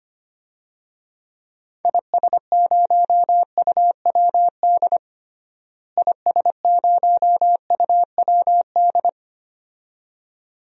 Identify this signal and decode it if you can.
Morse code, heard as a keyed tone, about 25 wpm, 700 Hz. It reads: IS0UWD IS0UWD
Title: DXCC-100 Call Signs - 6 Characters - Repeated 2x - 25wpm